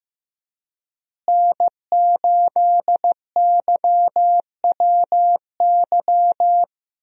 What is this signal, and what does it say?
Morse code, heard as a keyed tone, about 15 wpm, 700 Hz. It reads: N8YWY
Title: USA Call Signs - 5 Characters - Repeated 3x - 15wpm